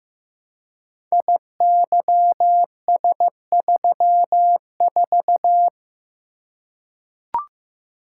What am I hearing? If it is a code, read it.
Morse code, heard as a keyed tone, about 15 wpm, 700 Hz. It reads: IYS34
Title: Code Groups - 5 Letter:Numbers - 15wpm